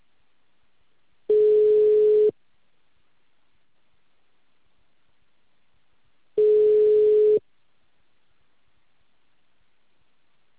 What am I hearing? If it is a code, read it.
Ringback tone (Europe, 425 Hz).